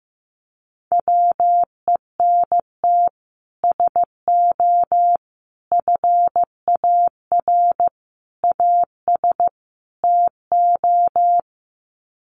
Morse code, heard as a keyed tone, about 15 wpm, 700 Hz. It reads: WENT SO FAR AS TO
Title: Sets of 5 Words Encore 15wpm